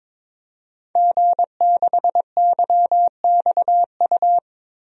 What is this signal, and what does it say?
Morse code, heard as a keyed tone, about 22 wpm, 700 Hz. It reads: G6YXU
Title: DXCC-100 Call Signs - 5 Characters - 22wpm